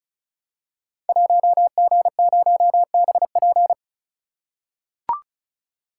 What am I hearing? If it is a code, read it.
Morse code, heard as a keyed tone, about 36 wpm, 690 Hz. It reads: 1G0BP
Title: Code Groups - 5 Letter:Numbers - 35wpm